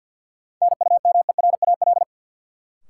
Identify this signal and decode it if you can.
Morse code, heard as a keyed tone, about 52 wpm, 700 Hz. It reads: NUMERAL
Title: Sentences from Top 1000 Words - Review Part 2 - New Word Drill - 50wpm